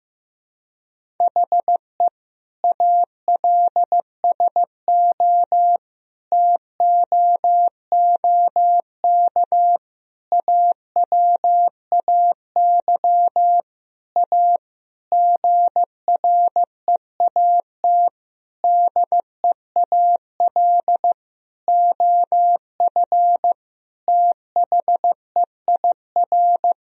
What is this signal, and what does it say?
Morse code, heard as a keyed tone, about 15 wpm, 700 Hz. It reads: HE ALSO TOOK AWAY A GREAT DEAL OF THEIR
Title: Sentences from Top 1000 Words - Review Part 1 - 15wpm